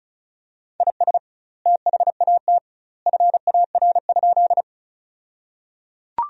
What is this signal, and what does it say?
Morse code, heard as a keyed tone, about 34 wpm, 700 Hz. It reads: IS THAT FAR?
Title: 2x Word-Spacing - Sentences from Top 300 Words - 35wpm